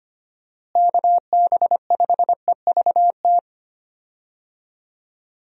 Morse code, heard as a keyed tone, about 25 wpm, 700 Hz. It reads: KB5EVT
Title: USA Call Signs - 6 Characters - Repeated 3x - 25wpm